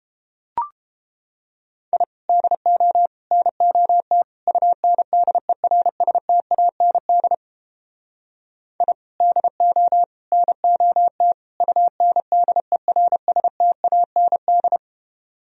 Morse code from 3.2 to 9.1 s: NOT UNDERSTAND I

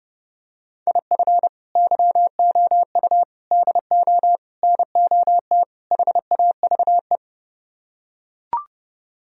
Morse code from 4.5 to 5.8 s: NOT

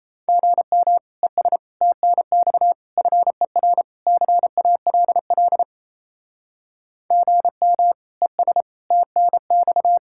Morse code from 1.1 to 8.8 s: ES TNX FER CALL GM ES

